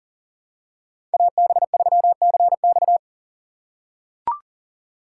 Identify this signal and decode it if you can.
Morse code, heard as a keyed tone, about 38 wpm, 700 Hz. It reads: AB3CX
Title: USA Call Signs - 2x Character Spacing - 40wpm